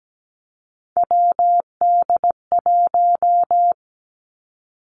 Morse code, heard as a keyed tone, about 17 wpm, 700 Hz. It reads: WD1
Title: Three Challenging Letter-Number Combinations 17wpm